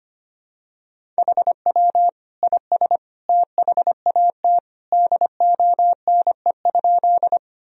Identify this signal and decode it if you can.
Morse code, heard as a keyed tone, about 25 wpm, 700 Hz. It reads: HW IS THAT DONE?
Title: Sentences from Top 500 Words and Abbreviations 25wpm